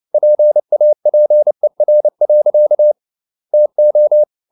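Morse code, heard as a keyed tone, about 29 wpm, 600 Hz. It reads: PAPER. TO
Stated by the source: Morse code at 29 wpm, sending Ken Follett's novel Never in English, from CW med Gnister.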